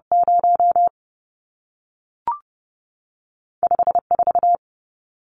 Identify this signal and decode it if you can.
Morse code, heard as a keyed tone, about 30 wpm, 700 Hz. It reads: 0 5 4
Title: Numbers - 2-Digit 30wpm